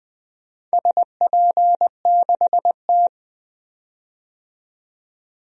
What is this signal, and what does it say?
Morse code, heard as a keyed tone, about 20 wpm, 700 Hz. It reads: SP6T
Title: DXCC-100 Call Signs - 4 Characters - 20wpm